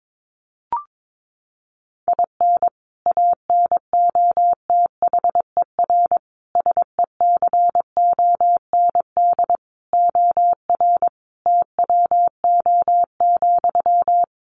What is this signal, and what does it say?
Morse code, heard as a keyed tone, about 22 wpm, 700 Hz. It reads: IN ANOTHER SECOND OR TWO,
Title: Sentences from Top 1500 Words 22wpm